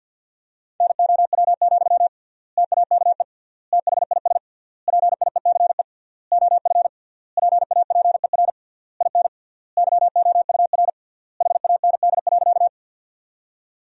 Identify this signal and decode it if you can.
Morse code, heard as a keyed tone, about 51 wpm, 700 Hz. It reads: NOW, TAKE THIS PIECE OF PAPER IN YOUR HAND.
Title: 3x Word-Spacing - Sentences from Top 400 Words - 50wpm